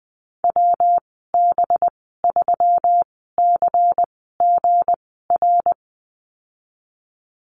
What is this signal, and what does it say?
Morse code, heard as a keyed tone, about 19 wpm, spaced as for 14 wpm, 700 Hz. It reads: WB3CGR
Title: 186 Lesson 30g - Letter b - Call Signs 20wpm